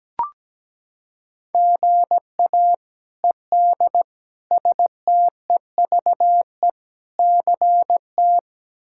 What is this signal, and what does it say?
Morse code, heard as a keyed tone, about 17 wpm, 700 Hz. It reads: GA ED STEVE CT
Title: K1USN SST QSO Practice - 17wpm